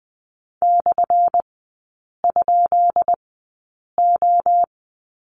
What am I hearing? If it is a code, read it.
Morse code, heard as a keyed tone, about 20 wpm, spaced as for 8 wpm, 700 Hz. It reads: /?O
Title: 178 Lesson 29e - Slash - Three Characters 20wpm